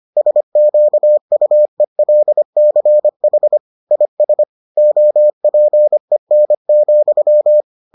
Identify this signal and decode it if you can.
Morse code, heard as a keyed tone, about 25 wpm, 600 Hz. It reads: SQUELCH IS OPEN,